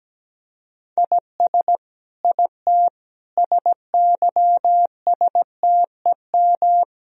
Morse code, heard as a keyed tone, about 17 wpm, 700 Hz. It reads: IS IT SYSTEM
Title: Sentences from Top 1000 Words - Review Part 3 - 17wpm